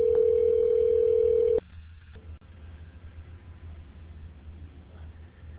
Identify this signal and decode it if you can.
Ringback tone (North America), 440 + 480 Hz.